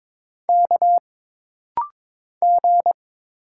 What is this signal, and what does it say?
Morse code, heard as a keyed tone, about 22 wpm, 700 Hz. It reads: K G
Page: This rapid-fire timing is geared to push your ICR (Instant Character Recognition) skills.